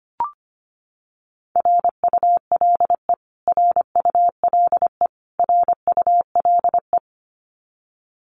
Morse code, heard as a keyed tone, about 25 wpm, 700 Hz. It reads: RULE RULE RULE
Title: Top 500 Words Repeated 25wpm